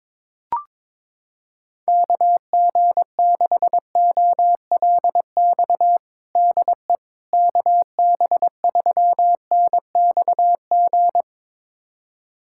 Morse code, heard as a keyed tone, about 22 wpm, 700 Hz. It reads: KG6OLX DE KB3NXG
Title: USA Callsign DE Callsign - 6 Characters - 22wpm